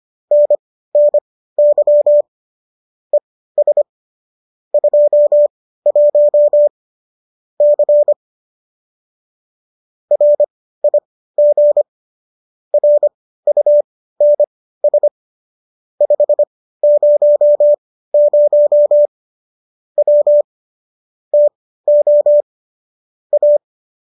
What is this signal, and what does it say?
Morse code, heard as a keyed tone, about 25 wpm, spaced as for 15 wpm, 590 Hz. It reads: NNY ES 21 C RIG RUNS 500 W TO A